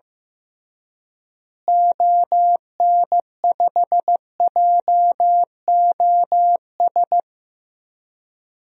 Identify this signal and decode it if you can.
Morse code, heard as a keyed tone, about 15 wpm, 700 Hz. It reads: ON5JOS
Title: DXCC-100 Call Signs - 6 Characters - Repeated 2x - 15wpm